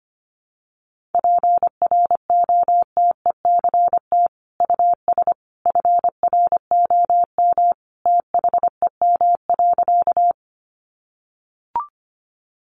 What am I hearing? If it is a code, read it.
Morse code, heard as a keyed tone, about 25 wpm, 700 Hz. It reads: PROTECT US FROM THEM.
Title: Sentences from Top 900 Words 25wpm